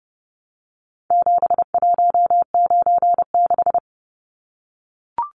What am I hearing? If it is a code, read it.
Morse code, heard as a keyed tone, about 30 wpm, 700 Hz. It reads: 7196
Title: Numbers - 4-Digit 30wpm